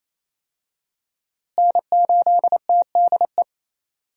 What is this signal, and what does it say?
Morse code, heard as a keyed tone, about 28 wpm, 700 Hz. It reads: N8TDE
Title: USA Call Signs - 5 Characters - Repeated 3x - 28wpm